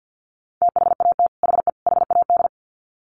Morse code, heard as a keyed tone, about 50 wpm, 700 Hz. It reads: THAT HE HAD
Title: Sets of 3 Words 50wpm